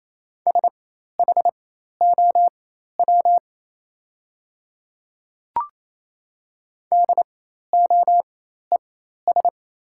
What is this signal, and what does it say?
Morse code, heard as a keyed tone, about 28 wpm, spaced as for 13 wpm, 690 Hz. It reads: SHOW DOES
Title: 4x Character Spacing - Top 200 Words - 28wpm